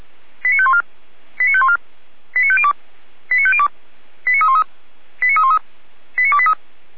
Five-tone selective call (ZVEI), heard as quick runs of five tones, tones of about 70 ms. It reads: 89514 89514 89502 89502 89313 89313 89283